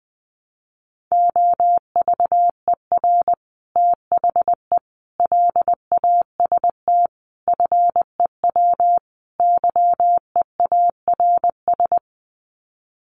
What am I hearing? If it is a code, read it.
Morse code, heard as a keyed tone, about 20 wpm, 700 Hz. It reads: OVER THE LAST FEW YEARS
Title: Sets of 5 Words Encore 20wpm